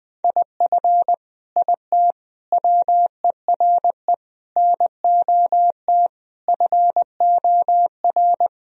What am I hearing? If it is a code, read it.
Morse code, heard as a keyed tone, about 20 wpm, 700 Hz. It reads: IF IT WERE NOT FOR
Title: Sentences from Top 1000 Words - Review Part 2 - 20wpm